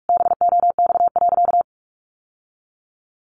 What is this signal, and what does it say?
Morse code, heard as a keyed tone, about 44 wpm, 700 Hz. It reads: BOX.
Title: Sentences from Top 700 Words 45wpm